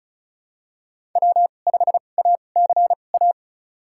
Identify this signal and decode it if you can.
Morse code, heard as a keyed tone, about 36 wpm, spaced as for 25 wpm, 700 Hz. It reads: W5ACA
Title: USA Call Signs - 5 Characters - 2x Character Spacing - 35wpm